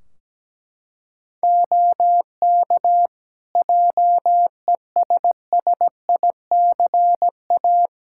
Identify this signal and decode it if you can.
Morse code, heard as a keyed tone, about 17 wpm, 700 Hz. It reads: OK JESSICA
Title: Practice QSO - Exchange 2 - Info - 17wpm